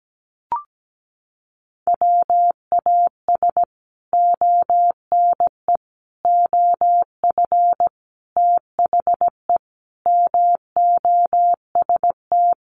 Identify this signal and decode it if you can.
Morse code, heard as a keyed tone, about 17 wpm, 700 Hz. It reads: WAS ONE OF THE MOST
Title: Sets of 5 Words 17wpm